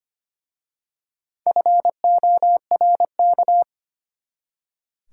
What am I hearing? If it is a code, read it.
Morse code, heard as a keyed tone, about 25 wpm, 700 Hz. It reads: FORK
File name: practice / 25wpm / Easy Words with Hints - 25wpm